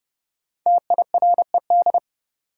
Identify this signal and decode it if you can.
Morse code, heard as a keyed tone, about 30 wpm, 700 Hz. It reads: TIRED
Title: Five Letter Words - 30wpm